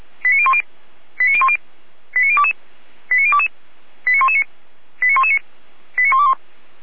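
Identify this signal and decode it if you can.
Five-tone selective call (ZVEI), heard as quick runs of five tones, tones of about 70 ms. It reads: 89010 89010 89022 89022 89119 89119 89121
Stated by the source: Five-tone ZVEI selective call received by SDR.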